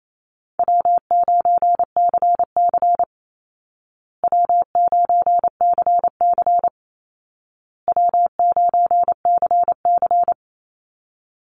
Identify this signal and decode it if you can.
Morse code, heard as a keyed tone, about 27 wpm, 700 Hz. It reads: W9CC W9CC W9CC
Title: Call Signs Repeated 2x - 28wpm